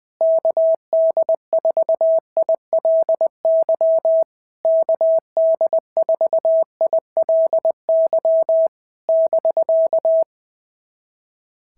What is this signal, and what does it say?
Morse code, heard as a keyed tone, about 19 wpm, 650 Hz. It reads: KD4ILY KD4ILY <BK>
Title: SOTA QSO Practice 20wpm